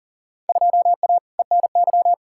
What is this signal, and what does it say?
Morse code, heard as a keyed tone, about 40 wpm, 700 Hz. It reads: 2A ENY
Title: Field Day Exchanges Encore - 40wpm